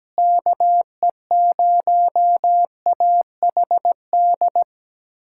Morse code, heard as a keyed tone, about 17 wpm, 700 Hz. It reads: KE0AHD
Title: USA Callsign DE Callsign 17wpm